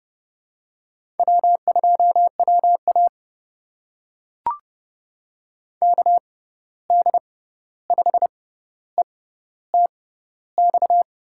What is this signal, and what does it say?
Morse code, heard as a keyed tone, about 30 wpm, spaced as for 14 wpm, 700 Hz. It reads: W2WA KD5ETX
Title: USA Call Signs - 6x Character Spacing - 30wpm